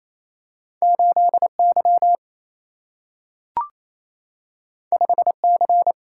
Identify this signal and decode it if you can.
Morse code, heard as a keyed tone, about 28 wpm, 700 Hz. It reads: 8Y 5C
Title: Single-Letter Single-Number 28wpm